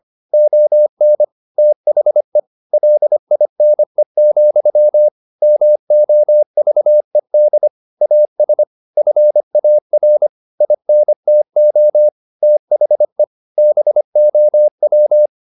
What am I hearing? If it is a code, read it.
Morse code, heard as a keyed tone, about 25 wpm, 600 Hz. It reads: ON THE LINE, MOVED AS FAR INTO THE BOW
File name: the old man and the sea part 1 25wpm